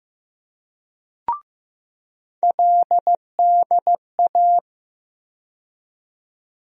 Morse code, heard as a keyed tone, about 15 wpm, 700 Hz. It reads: LDA
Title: USA Call Sign Postfixes - 3 Letters 15wpm